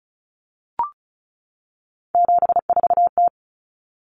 Morse code, heard as a keyed tone, about 34 wpm, 700 Hz. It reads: 74T
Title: Common Cut Numbers - 3-Digit 35wpm